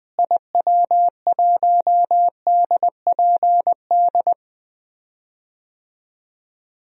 Morse code, heard as a keyed tone, about 20 wpm, 700 Hz. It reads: IW1DPD
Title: DXCC-100 Call Signs - 6 Characters - Repeated 2x - 20wpm